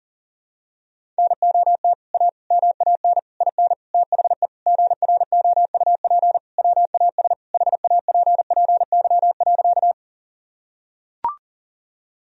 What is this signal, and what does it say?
Morse code, heard as a keyed tone, about 40 wpm, 700 Hz. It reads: NOT A MAN IN THE GROUP WAS HAPPY.
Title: Sentences from Top 1000 Words - Review Part 1 - 40wpm